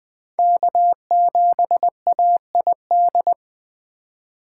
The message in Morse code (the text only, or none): K7AID